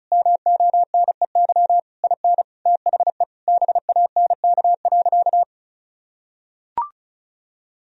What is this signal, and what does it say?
Morse code, heard as a keyed tone, about 34 wpm, 700 Hz. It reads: MONEY IN THE BANK.
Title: Sentences from Top 1000 Words - Review Part 4 - 35wpm